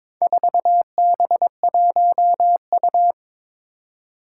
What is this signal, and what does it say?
Morse code, heard as a keyed tone, about 22 wpm, 700 Hz. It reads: 4B1U
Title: Four Challenging Letter-Number Combinations 22wpm